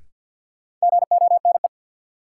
Morse code, heard as a keyed tone, about 52 wpm, 700 Hz. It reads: GONE
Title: Sentences from Top 1200 Words - New Word Drill Alt - 50wpm